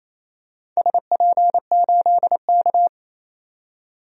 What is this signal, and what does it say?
Morse code, heard as a keyed tone, about 27 wpm, 700 Hz. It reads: SP8K
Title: DXCC-100 Call Signs - 4 Characters - 28wpm